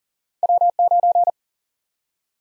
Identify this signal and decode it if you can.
Morse code, heard as a keyed tone, about 37 wpm, 700 Hz. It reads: W9
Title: USA Call Sign Prefixes - Repeated 2x - 40wpm